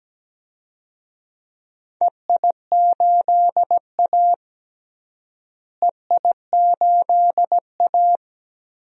Morse code, heard as a keyed tone, about 17 wpm, 700 Hz. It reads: EI8A EI8A
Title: DXCC-100 Call Signs - 4 Characters - Repeated 2x - 17wpm